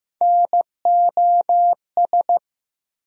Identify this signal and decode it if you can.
Morse code, heard as a keyed tone, about 15 wpm, 700 Hz. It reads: NOS
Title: USA Call Sign Postfixes - 3 Letters 15wpm